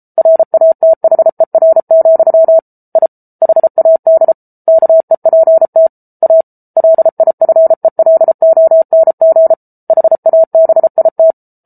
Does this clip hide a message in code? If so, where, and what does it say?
Yes: Morse code from 0.2 to 11.3 s, heard as a keyed tone, about 34 wpm, 650 Hz. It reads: RATHER, I HAD KEPT A LIFELONG HABIT